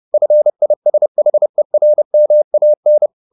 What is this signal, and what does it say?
Morse code, heard as a keyed tone, about 30 wpm, 600 Hz. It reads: FISHERMAN